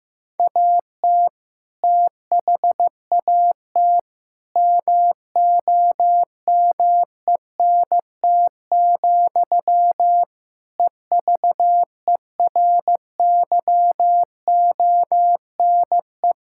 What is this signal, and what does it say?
Morse code, heard as a keyed tone, about 15 wpm, 700 Hz. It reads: AT THAT MOMENT, EVERYONE
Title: Sentences from Top 1000 Words - Review Part 5 - 15wpm